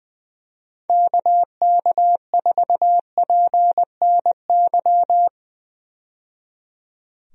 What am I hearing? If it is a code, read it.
Morse code, heard as a keyed tone, about 20 wpm, 700 Hz. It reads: KK4PNY
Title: USA Call Signs - 6 Characters - 20wpm